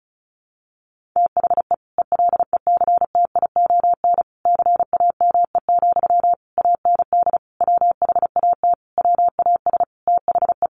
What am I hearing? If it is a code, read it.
Morse code, heard as a keyed tone, about 34 wpm, 700 Hz. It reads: THE ELECTION CAME, AND WHAT WAS THE
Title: Sentences from Top 1500 Words 35wpm